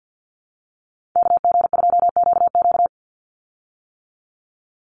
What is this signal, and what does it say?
Morse code, heard as a keyed tone, about 49 wpm, 700 Hz. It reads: KG2QQ
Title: USA Call Signs 50wpm